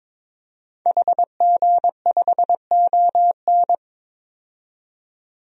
Morse code, heard as a keyed tone, about 22 wpm, 700 Hz. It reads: HG5ON
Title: DXCC-100 Call Signs - 5 Characters - 22wpm